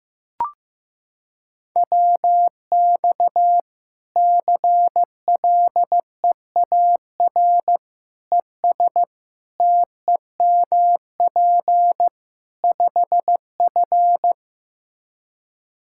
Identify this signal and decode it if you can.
Morse code, heard as a keyed tone, about 15 wpm, 700 Hz. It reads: WX CLEAR ES TEMP 5F
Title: Practice QSO - Exchange 2 - Info - 15wpm